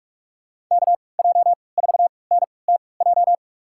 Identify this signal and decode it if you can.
Morse code, heard as a keyed tone, about 47 wpm, spaced as for 26 wpm, 700 Hz. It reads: KJ4NTJ
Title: USA Call Signs - 6 Characters - 3x Character Spacing - 45wpm